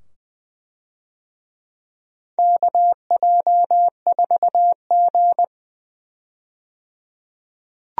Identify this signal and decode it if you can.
Morse code, heard as a keyed tone, about 20 wpm, 700 Hz. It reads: KJ4G
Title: USA Call Signs - 4 Characters - 20wpm